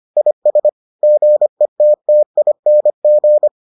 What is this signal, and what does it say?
Morse code, heard as a keyed tone, about 25 wpm, 600 Hz. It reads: IS GETTING